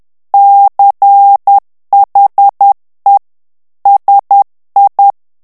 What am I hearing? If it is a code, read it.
Morse code, heard as a keyed tone, about 10 wpm, 800 Hz. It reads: CHE SI
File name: morse mp3 / ITALIEN texte clair 1 vitesse 10 mots minute